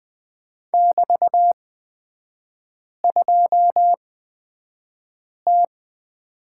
Morse code, heard as a keyed tone, about 20 wpm, spaced as for 5 wpm, 700 Hz. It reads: <BT>2T
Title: 224 Lesson 36e - BT - Three Characters 20wpm